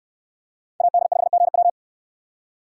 Morse code, heard as a keyed tone, about 66 wpm, 700 Hz. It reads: AD5GW